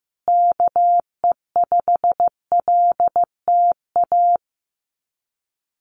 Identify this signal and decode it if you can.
Morse code, heard as a keyed tone, about 14 wpm, 700 Hz. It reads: KE5LTA
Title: USA Call Signs Repeated 3x 15wpm